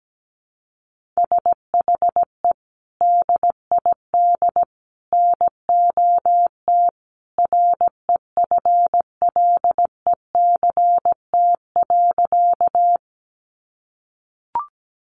Morse code, heard as a keyed tone, about 17 wpm, 700 Hz. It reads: SHE DID NOT REFLECT.
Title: Sentences from Top 1500 Words 17wpm